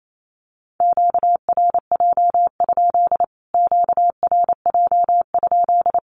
Morse code, heard as a keyed tone, about 28 wpm, 700 Hz. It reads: QRJ? QRJ?
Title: Q-Codes Repeated 3x 28wpm